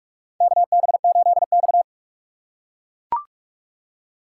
Morse code, heard as a keyed tone, about 41 wpm, 700 Hz. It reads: KB8X
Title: USA Call Signs - 4x Character Spacing - 45wpm